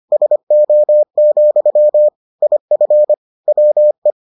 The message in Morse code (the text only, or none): SO, IF WE